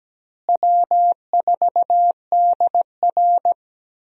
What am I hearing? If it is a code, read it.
Morse code, heard as a keyed tone, about 17 wpm, 700 Hz. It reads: W4DR
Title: USA Call Signs - 4 Characters - Repeated 2x - 17wpm